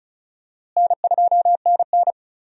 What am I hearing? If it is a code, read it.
Morse code, heard as a keyed tone, about 34 wpm, 700 Hz. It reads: N2NN
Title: USA Call Signs - 4 Characters - Repeated 3x - 35wpm